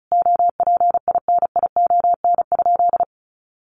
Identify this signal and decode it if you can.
Morse code, heard as a keyed tone, about 34 wpm, 700 Hz. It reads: OPINION?
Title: Sentences from Top 1200 Words 35wpm